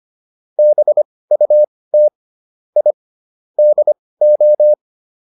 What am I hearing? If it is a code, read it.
Morse code, heard as a keyed tone, about 25 wpm, spaced as for 18 wpm, 600 Hz. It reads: BUT I DO